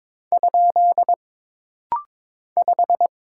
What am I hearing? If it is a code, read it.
Morse code, heard as a keyed tone, about 22 wpm, 700 Hz. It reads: ? 5
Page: This rapid-fire timing is geared to push your ICR (Instant Character Recognition) skills.